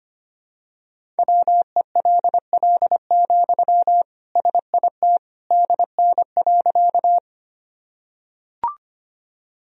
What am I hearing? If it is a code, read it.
Morse code, heard as a keyed tone, about 25 wpm, 700 Hz. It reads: WELL, SIT DN.
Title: Sentences from Top 500 Words and Abbreviations 25wpm